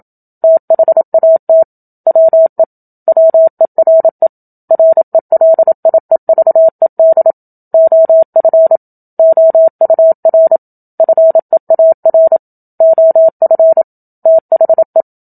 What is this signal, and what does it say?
Morse code, heard as a keyed tone, about 27 wpm, 650 Hz. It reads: THAT WE WERE RELIEVED OF OUR FEAR OF THE
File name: Book Two / 01-Chapter 1 - 27 WPM